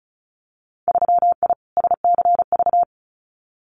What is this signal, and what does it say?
Morse code, heard as a keyed tone, about 34 wpm, 700 Hz. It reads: 3I SCV
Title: Winter Field Day Exchanges Encore 35wpm